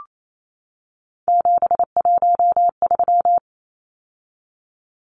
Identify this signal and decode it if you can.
Morse code, heard as a keyed tone, about 28 wpm, 700 Hz. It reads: 713
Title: Numbers - 3-Digit 28wpm